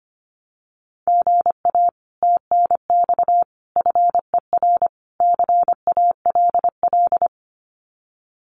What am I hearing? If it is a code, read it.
Morse code, heard as a keyed tone, about 25 wpm, 700 Hz. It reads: GA TNX FER CALL
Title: Common QSO Phrases Encore 25wpm